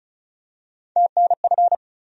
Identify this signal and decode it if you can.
Morse code, heard as a keyed tone, about 36 wpm, 700 Hz. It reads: TNF
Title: USA Call Sign Postfixes - 3 Letters 35wpm